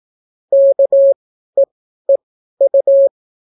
Morse code, heard as a keyed tone, about 18 wpm, spaced as for 12 wpm, 550 Hz. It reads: KEEU